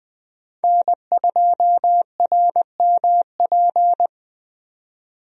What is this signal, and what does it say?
Morse code, heard as a keyed tone, about 20 wpm, 700 Hz. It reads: N2RMP
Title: USA Call Signs - 5 Characters - 20wpm